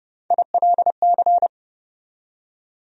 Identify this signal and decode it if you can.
Morse code, heard as a keyed tone, about 30 wpm, 700 Hz. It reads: ILC
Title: USA Call Sign Postfixes - 3 Letters 30wpm